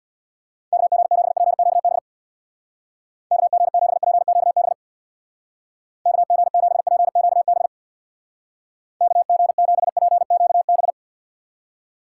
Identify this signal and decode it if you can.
Morse code, heard as a keyed tone, about 65 wpm, 700 Hz. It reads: KG7PQB KG7PQB KG7PQB KG7PQB